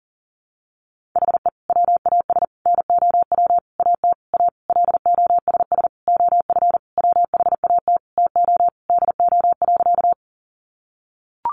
Morse code, heard as a keyed tone, about 40 wpm, 700 Hz. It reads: HE WAS NOW AT A LOSS OF WHAT TO DO.
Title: Sentences from Top 1400 Words 40wpm